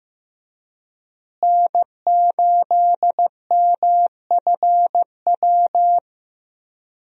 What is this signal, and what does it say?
Morse code, heard as a keyed tone, about 15 wpm, 700 Hz. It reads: N8MFW
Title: USA Call Signs - 5 Characters - Repeated 3x - 15wpm